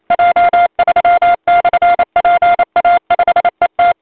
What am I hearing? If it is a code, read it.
Morse code, heard as a keyed tone, about 29 wpm, 700 Hz. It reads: J3/PA5ET
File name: voice / 80m j3 pa5et 002 2001 08 08